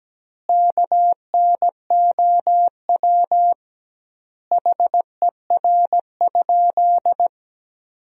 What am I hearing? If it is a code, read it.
Morse code, heard as a keyed tone, about 17 wpm, 700 Hz. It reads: KNOW HER?
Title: 2x Word-Spacing - Sentences from Top 100 Words - 17wpm